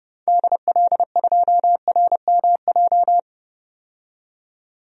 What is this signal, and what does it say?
Morse code, heard as a keyed tone, about 30 wpm, 700 Hz. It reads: DL2RMJ
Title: DXCC-100 Call Signs - 6 Characters - 30wpm